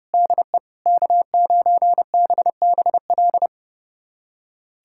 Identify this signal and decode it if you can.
Morse code, heard as a keyed tone, about 30 wpm, 700 Hz. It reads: DE K9BBL
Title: USA Callsign DE Callsign - 5 Characters - 30wpm